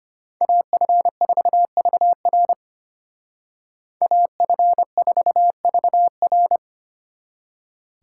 Morse code, heard as a keyed tone, about 33 wpm, 700 Hz. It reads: AF4VR AF4VR